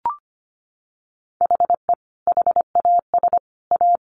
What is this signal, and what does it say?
Morse code, heard as a keyed tone, about 25 wpm, 700 Hz. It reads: HE HAS A
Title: Sentences from Top 1400 Words 25wpm